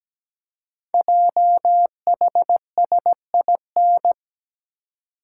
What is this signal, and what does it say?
Morse code, heard as a keyed tone, about 17 wpm, 700 Hz. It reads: JHSIN
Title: Code Groups - 5 Letters - 17wpm